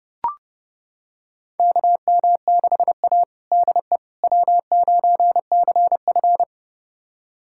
Morse code, heard as a keyed tone, about 30 wpm, 700 Hz. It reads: KM6A DE W9CF
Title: USA Callsign DE Callsign - 4 Characters - 30wpm